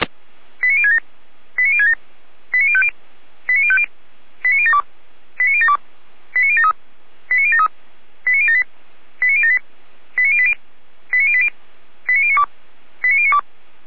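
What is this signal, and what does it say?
Five-tone selective call (ZVEI), heard as quick runs of five tones, tones of about 80 ms. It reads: 89067 89067 89050 89050 89072 89072 89073 89073 89078 89078 89080 89080 89092 89092
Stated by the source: ZVEI five-tone call received by SDR.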